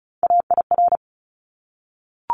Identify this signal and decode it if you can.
Morse code, heard as a keyed tone, about 34 wpm, 700 Hz. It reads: AIR
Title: Top 200 Words 35wpm